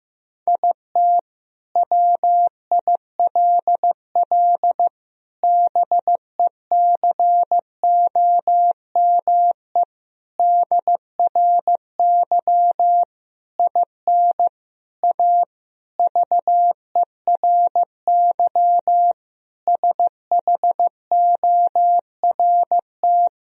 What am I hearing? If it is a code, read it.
Morse code, heard as a keyed tone, about 15 wpm, 700 Hz. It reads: IT WILL BECOME DRY IN A VERY SHORT
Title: Sentences from Top 1000 Words - Review Part 3 - 15wpm